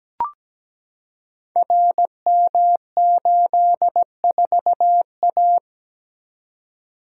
Morse code, heard as a keyed tone, about 17 wpm, 700 Hz. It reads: RM84A
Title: Code Groups - 5 Letter:Numbers - 17wpm